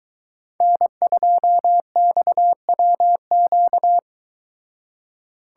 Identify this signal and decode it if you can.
Morse code, heard as a keyed tone, about 24 wpm, 700 Hz. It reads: N2XWQ